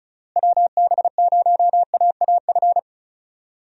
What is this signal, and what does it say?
Morse code, heard as a keyed tone, about 36 wpm, 700 Hz. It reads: WB0AAF
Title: USA Call Signs - 6 Characters - Repeated 3x - 35wpm